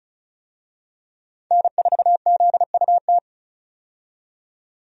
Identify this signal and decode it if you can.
Morse code, heard as a keyed tone, about 34 wpm, 700 Hz. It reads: N4ZUT
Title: USA Call Signs - 5 Characters - 35wpm